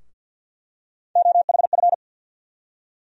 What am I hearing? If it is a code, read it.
Morse code, heard as a keyed tone, about 51 wpm, 700 Hz. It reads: OHF
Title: USA Call Sign Postfixes - 3 Letters 50wpm